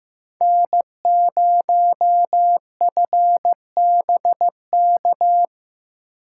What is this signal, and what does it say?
Morse code, heard as a keyed tone, about 15 wpm, 700 Hz. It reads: N0FBK
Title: USA Call Signs - 5 Characters - 6x Character Spacing - 15wpm